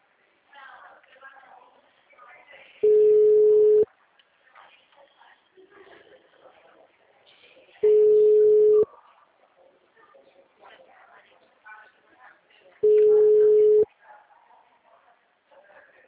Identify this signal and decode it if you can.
Ringback tone (Europe, 425 Hz), with speech.